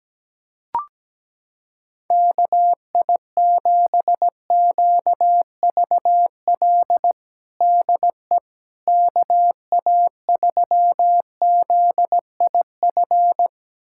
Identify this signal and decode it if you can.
Morse code, heard as a keyed tone, about 17 wpm, 700 Hz. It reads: KI7QVL DE KA3ZIF
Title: USA Callsign DE Callsign 17wpm